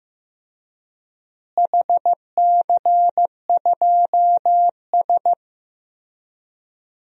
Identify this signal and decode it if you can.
Morse code, heard as a keyed tone, about 16 wpm, 700 Hz. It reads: HC2S
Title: DXCC-100 Call Signs - 4 Characters - Repeated 3x - 15wpm